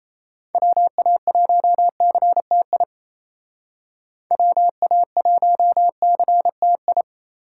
Morse code, heard as a keyed tone, about 36 wpm, 700 Hz. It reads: WA1CTI WA1CTI